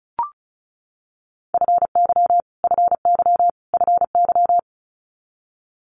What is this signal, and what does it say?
Morse code, heard as a keyed tone, about 36 wpm, 700 Hz. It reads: FY FY FY
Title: Multi-letter Phonemes Repeated 3x 35wpm